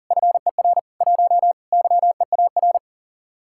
Morse code, heard as a keyed tone, about 50 wpm, 700 Hz. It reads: FER 1 YEAR